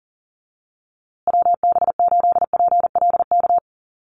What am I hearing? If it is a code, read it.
Morse code, heard as a keyed tone, about 40 wpm, 700 Hz. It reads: WB8PLK
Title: USA Call Signs Repeated 2x 40wpm